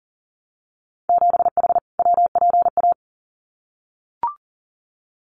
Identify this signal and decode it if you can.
Morse code, heard as a keyed tone, about 40 wpm, 700 Hz. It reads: 7H WPA
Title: Winter Field Day Exchanges Encore 40wpm